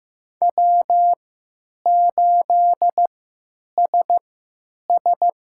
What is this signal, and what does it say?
Morse code, heard as a keyed tone, about 15 wpm, spaced as for 8 wpm, 700 Hz. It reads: W8SS
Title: USA Call Signs - 3x Character Spacing - 15wpm